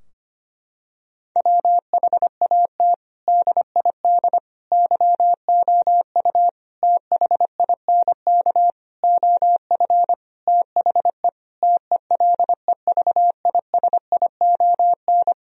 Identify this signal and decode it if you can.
Morse code, heard as a keyed tone, about 25 wpm, 700 Hz. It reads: WHAT DID YOU THINK OF THE TELEVISION
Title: Sentences from Top 1000 Words - Review Part 4 - 25wpm